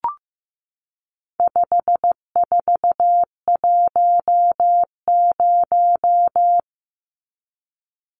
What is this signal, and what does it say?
Morse code, heard as a keyed tone, about 14 wpm, 700 Hz. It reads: 5410
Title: Numbers - 4-Digit 15wpm